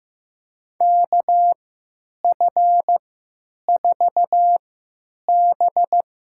Morse code, heard as a keyed tone, about 15 wpm, spaced as for 9 wpm, 700 Hz. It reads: KF4B
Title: USA Call Signs - 3x Character Spacing - 15wpm